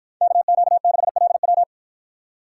Morse code, heard as a keyed tone, about 53 wpm, 700 Hz. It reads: KQ6LW